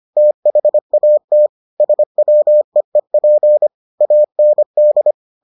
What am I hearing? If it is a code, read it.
Morse code, heard as a keyed tone, about 25 wpm, 600 Hz. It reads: THAT SWEEP AND